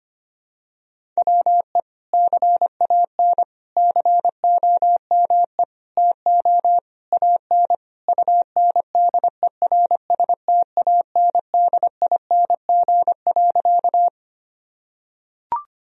Morse code, heard as a keyed tone, about 25 wpm, 700 Hz. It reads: WE CAN COME TO AN UNDERSTANDING.
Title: Sentences from Top 1000 Words - Review Part 3 - 25wpm